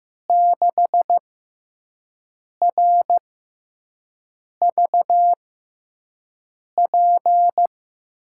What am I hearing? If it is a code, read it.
Morse code, heard as a keyed tone, about 15 wpm, spaced as for 5 wpm, 700 Hz. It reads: 6RVP
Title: USA Call Signs - 6 Characters - 6x Character Spacing - 15wpm